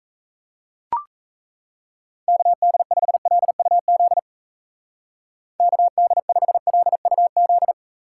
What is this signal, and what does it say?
Morse code, heard as a keyed tone, about 37 wpm, 700 Hz. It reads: KD5LUZ KD5LUZ